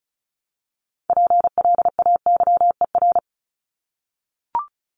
Morse code, heard as a keyed tone, about 34 wpm, 700 Hz. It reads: PLAYER
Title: Top 501-750 Words 35wpm